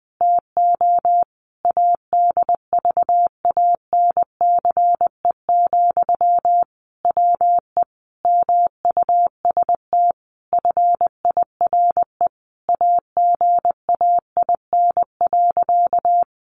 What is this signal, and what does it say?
Morse code, heard as a keyed tone, about 20 wpm, 700 Hz. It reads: TO ADVANCE, WE MUST FIRE AGAIN.
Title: Sentences from Top 1400 Words 20wpm